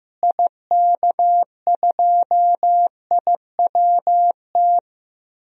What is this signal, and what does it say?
Morse code, heard as a keyed tone, about 15 wpm, 700 Hz. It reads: IK2IWT
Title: DXCC-100 Call Signs - 6 Characters - Repeated 3x - 15wpm